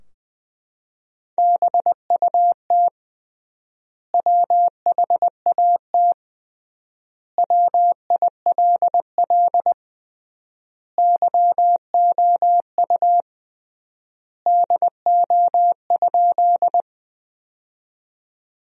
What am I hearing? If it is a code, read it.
Morse code, heard as a keyed tone, about 20 wpm, 700 Hz. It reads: BUT WHAT WILL YOU DO?
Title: 3x Word-Spacing - Sentences from Top 100 Words - 20wpm